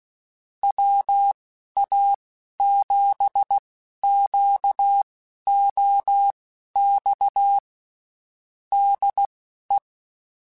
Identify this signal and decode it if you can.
Morse code, heard as a keyed tone, about 16 wpm, spaced as for 11 wpm, 800 Hz. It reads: WA7QOX DE
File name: (spaced) / QSOs Spaced-mp3s / WA7QOX 1 @15 sp